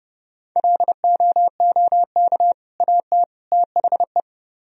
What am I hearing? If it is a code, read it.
Morse code, heard as a keyed tone, about 27 wpm, 700 Hz. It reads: LOOK AT THE